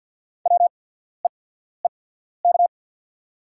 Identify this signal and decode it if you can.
Morse code, heard as a keyed tone, about 48 wpm, spaced as for 14 wpm, 690 Hz. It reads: WEEK
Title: 8x Character Spacing - Top 500 Words - 50wpm